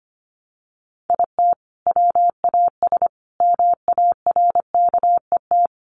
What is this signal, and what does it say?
Morse code, heard as a keyed tone, about 25 wpm, 700 Hz. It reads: IT WAS MARKET
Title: Sentences from Top 600 Words 25wpm